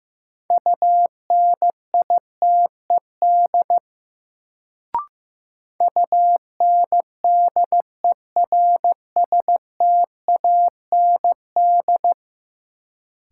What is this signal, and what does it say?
Morse code, heard as a keyed tone, about 15 wpm, 700 Hz. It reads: UNITED UNDERSTAND
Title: Sentences from Top 600 Words - New Word Drill - 15wpm